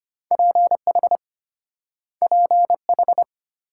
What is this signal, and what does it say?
Morse code, heard as a keyed tone, about 33 wpm, 700 Hz. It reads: PH PH